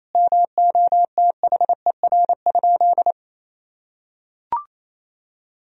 Morse code, heard as a keyed tone, about 28 wpm, 700 Hz. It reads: MOTHER?
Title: Sentences from Top 1000 Words - Review Part 2 - 28wpm